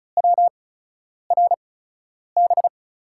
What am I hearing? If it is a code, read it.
Morse code, heard as a keyed tone, about 34 wpm, spaced as for 10 wpm, 700 Hz. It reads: WRB
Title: USA Call Signs - 5 Characters - 8x Character Spacing - 35wpm